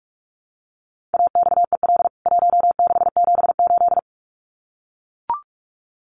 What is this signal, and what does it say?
Morse code, heard as a keyed tone, about 44 wpm, 700 Hz. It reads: AXEL 1678
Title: CWT Exchange 45wpm